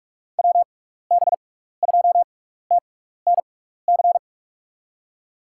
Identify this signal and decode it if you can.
Morse code, heard as a keyed tone, about 46 wpm, spaced as for 16 wpm, 700 Hz. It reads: WB2TNC
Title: USA Call Signs - 6 Characters - 6x Character Spacing - 45wpm